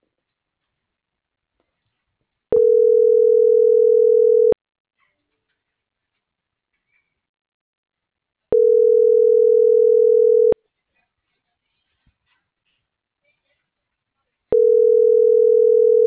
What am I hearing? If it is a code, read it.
Ringback tone (North America), 440 + 480 Hz.